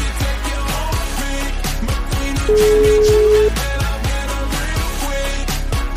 Ringback tone (Europe, 425 Hz), with noise.